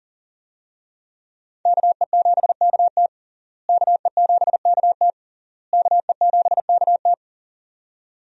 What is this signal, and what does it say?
Morse code, heard as a keyed tone, about 40 wpm, 700 Hz. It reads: KE7KT KE7KT KE7KT
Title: USA Call Signs - 5 Characters - Repeated 3x - 40wpm